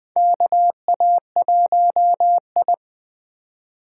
Morse code, heard as a keyed tone, about 21 wpm, 700 Hz. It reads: KA1I
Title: USA Call Signs - 3x Character Spacing - 20wpm